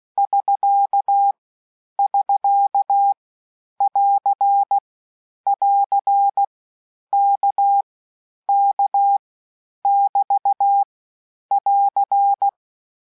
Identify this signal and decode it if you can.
Morse code, heard as a keyed tone, about 16 wpm, 800 Hz. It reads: <SK> <SK> <AR> <AR> K K <BT> <AR>
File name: cours-du-lundi files / cours-1580-abreviations-mpm-15